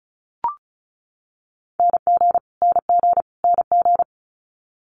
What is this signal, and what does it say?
Morse code, heard as a keyed tone, about 36 wpm, 700 Hz. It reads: NG NG NG
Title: Multi-letter Phonemes Repeated 3x 35wpm